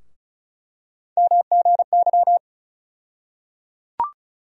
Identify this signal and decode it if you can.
Morse code, heard as a keyed tone, about 34 wpm, 700 Hz. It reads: MGY
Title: USA Call Sign Postfixes - 3 Letters 35wpm